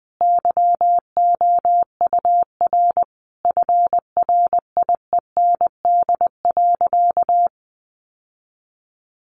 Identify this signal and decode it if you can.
Morse code, heard as a keyed tone, about 19 wpm, 700 Hz. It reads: YOUR FRIEND.
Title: Sentences from Top 1300 Words 20wpm